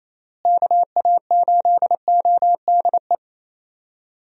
Morse code, heard as a keyed tone, about 28 wpm, 700 Hz. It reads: KA8ODE
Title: USA Call Signs - 6 Characters - Repeated 2x - 28wpm